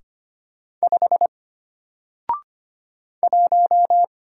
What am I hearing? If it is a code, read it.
Morse code, heard as a keyed tone, about 25 wpm, 700 Hz. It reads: 5 1
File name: Single-Letter-Number-25wpm